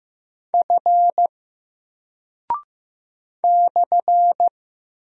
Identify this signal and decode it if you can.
Morse code, heard as a keyed tone, about 15 wpm, 700 Hz. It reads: F /
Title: Single Character (Letter:Number:Punctuation:Prosign) 15wpm